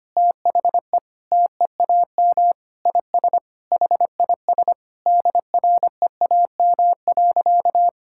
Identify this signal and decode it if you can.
Morse code, heard as a keyed tone, about 25 wpm, 700 Hz. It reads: THE TEAM IS HIS DREAM.
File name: practice / 25wpm / Sentences with 4-Word Phrases - 25wpm